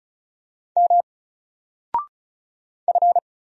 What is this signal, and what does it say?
Morse code, heard as a keyed tone, about 36 wpm, 700 Hz. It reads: M F
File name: Single-Letter-Number-35wpm